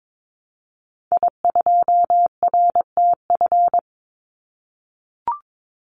Morse code, heard as a keyed tone, about 22 wpm, 700 Hz. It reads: I2RTF
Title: Call Signs - 22wpm